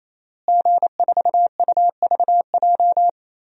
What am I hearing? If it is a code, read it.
Morse code, heard as a keyed tone, about 27 wpm, 700 Hz. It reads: G4UVJ
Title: DXCC-100 Call Signs - 5 Characters - Repeated 3x - 28wpm